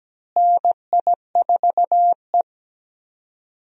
Morse code, heard as a keyed tone, about 17 wpm, 700 Hz. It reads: NI4E
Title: USA Call Signs - 4 Characters - Repeated 2x - 17wpm